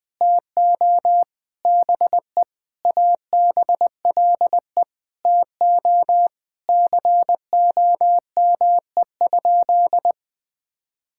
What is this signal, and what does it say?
Morse code, heard as a keyed tone, about 20 wpm, 700 Hz. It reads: TO BE ABLE TO COME?
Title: Sentences with 5-Word Phrases - 20wpm